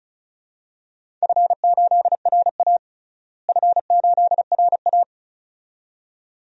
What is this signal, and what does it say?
Morse code, heard as a keyed tone, about 34 wpm, 700 Hz. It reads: F8RA F8RA
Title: DXCC-100 Call Signs - 4 Characters - Repeated 2x - 35wpm